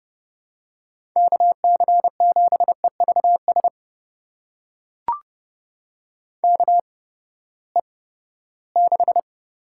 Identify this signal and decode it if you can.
Morse code, heard as a keyed tone, about 30 wpm, spaced as for 17 wpm, 700 Hz. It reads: KC7EVS KE6
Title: USA Call Signs - 6 Characters - 8x Character Spacing - 30wpm